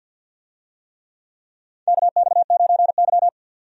Morse code, heard as a keyed tone, about 52 wpm, 700 Hz. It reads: KX9Y
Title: USA Call Signs - 4 Characters - 50wpm